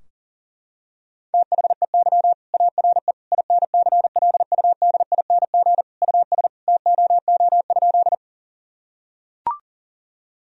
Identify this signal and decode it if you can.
Morse code, heard as a keyed tone, about 37 wpm, 700 Hz. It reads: THEY ARE INCLUDING US TOO?
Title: Sentences from Top 1000 Words - Review Part 4 - 40wpm